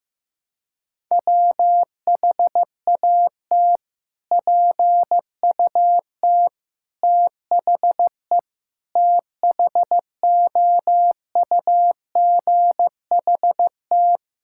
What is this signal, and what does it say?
Morse code, heard as a keyed tone, about 15 wpm, 700 Hz. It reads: WHAT PUT THE THOUGHT
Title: Sentences from Top 1000 Words - Review Part 5 - 15wpm